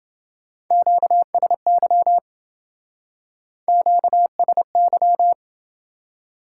Morse code, heard as a keyed tone, about 29 wpm, 700 Hz. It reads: QSY QSY